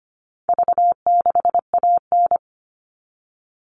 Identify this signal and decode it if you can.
Morse code, heard as a keyed tone, about 25 wpm, 700 Hz. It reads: V6AN
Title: All Cut Numbers - 4-Digit 25wpm